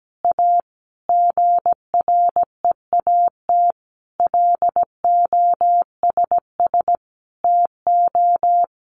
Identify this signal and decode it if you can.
Morse code, heard as a keyed tone, about 17 wpm, 700 Hz. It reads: A GREAT LOSS TO
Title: Sentences from Top 1400 Words 17wpm